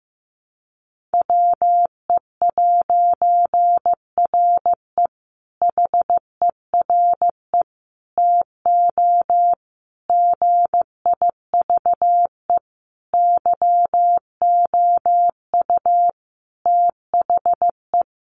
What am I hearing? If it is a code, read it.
Morse code, heard as a keyed tone, about 15 wpm, 700 Hz. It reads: WE'RE HERE TO GIVE YOU THE
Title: A Sound Of Thunder 15wpm